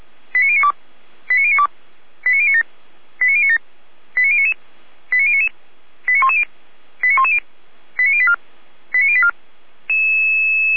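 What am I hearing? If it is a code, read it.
Five-tone selective call (ZVEI), heard as quick runs of five tones, tones of about 70 ms. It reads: 89092 89092 89097 89097 89099 89099 89110 89110 89084 89084
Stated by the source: ZVEI five-tone selective call 89078 received on an SDR.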